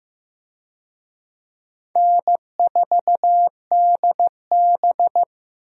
Morse code, heard as a keyed tone, about 15 wpm, 700 Hz. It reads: N4DB
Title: USA Call Signs - 4 Characters - Repeated 2x - 15wpm